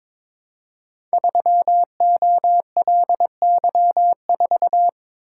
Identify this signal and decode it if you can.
Morse code, heard as a keyed tone, about 22 wpm, 700 Hz. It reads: 3OLY4
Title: Code Groups - 5 Letter:Numbers - 22wpm